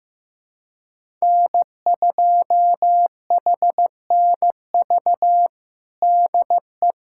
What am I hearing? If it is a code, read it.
Morse code, heard as a keyed tone, about 15 wpm, 700 Hz. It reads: N2HNV DE
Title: USA Callsign DE Callsign - 5 Characters - 15wpm